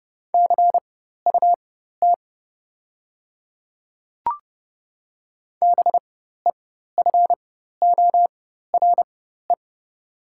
Morse code, heard as a keyed tone, about 30 wpm, spaced as for 14 wpm, 700 Hz. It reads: CUT BEFORE
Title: 4x Character Spacing - Top 500 Words - 30wpm